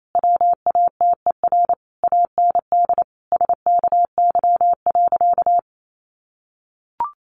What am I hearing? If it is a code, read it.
Morse code, heard as a keyed tone, about 28 wpm, 700 Hz. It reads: WATER AND SKY.
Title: Sentences from Top 1200 Words 28wpm